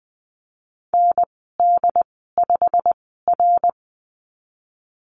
Morse code, heard as a keyed tone, about 20 wpm, spaced as for 15 wpm, 700 Hz. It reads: ND5R
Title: 088 Lesson 15g - Number 5 - Call Signs 20wpm